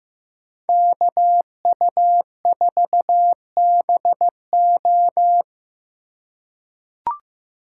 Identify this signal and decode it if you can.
Morse code, heard as a keyed tone, about 15 wpm, 700 Hz. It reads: KU4BO
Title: USA Call Signs - 5 Characters - 2x Character Spacing - 15wpm